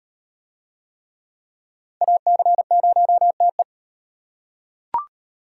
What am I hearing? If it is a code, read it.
Morse code, heard as a keyed tone, about 38 wpm, 700 Hz. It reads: AC0TE